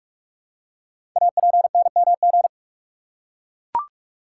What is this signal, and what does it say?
Morse code, heard as a keyed tone, about 44 wpm, 700 Hz. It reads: APNMG
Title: Code Groups - 5 Letters - 45wpm